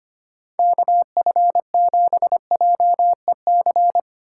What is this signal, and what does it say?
Morse code, heard as a keyed tone, about 33 wpm, 700 Hz. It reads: KF7JEC